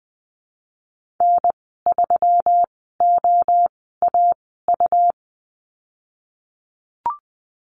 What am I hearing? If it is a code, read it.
Morse code, heard as a keyed tone, about 20 wpm, spaced as for 15 wpm, 700 Hz. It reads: N3OAU
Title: 128 Lesson 21g - Number 6 - Call Signs 20wpm